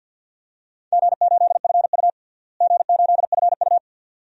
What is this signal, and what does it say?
Morse code, heard as a keyed tone, about 52 wpm, 700 Hz. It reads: G8FU G8FU
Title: DXCC-100 Call Signs - 4 Characters - Repeated 2x - 50wpm